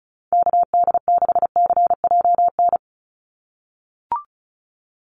Morse code, heard as a keyed tone, about 34 wpm, 700 Hz. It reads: KD6CJN
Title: USA Call Signs 35wpm